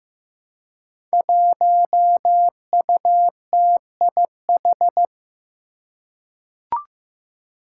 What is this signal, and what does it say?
Morse code, heard as a keyed tone, about 15 wpm, 700 Hz. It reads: 1UTIH
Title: Five Challenging Letter-Number Combinations 15wpm